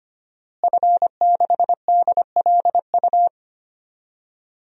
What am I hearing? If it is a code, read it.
Morse code, heard as a keyed tone, about 25 wpm, 700 Hz. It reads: F6DLU
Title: DXCC-100 Call Signs - 5 Characters - Repeated 2x - 25wpm